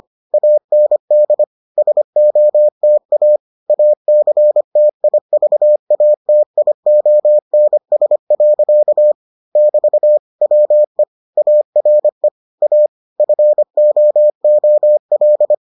Morse code, heard as a keyed tone, about 25 wpm, 600 Hz. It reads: AND SOTA ACTIVATIONS. <BT> WE ARE A FOOL